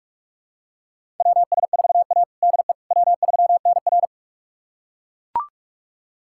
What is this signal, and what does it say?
Morse code, heard as a keyed tone, about 41 wpm, 700 Hz. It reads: WS4A DE W3NR
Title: USA Callsign DE Callsign - 4 Characters - 45wpm